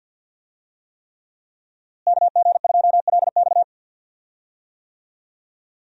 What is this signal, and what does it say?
Morse code, heard as a keyed tone, about 46 wpm, 700 Hz. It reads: KG2LX
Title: USA Call Signs - 5 Characters - Repeated 3x - 50wpm